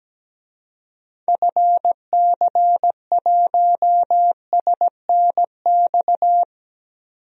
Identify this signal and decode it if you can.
Morse code, heard as a keyed tone, about 17 wpm, 700 Hz. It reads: FC1SNX
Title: DXCC-100 Call Signs - 6 Characters - 17wpm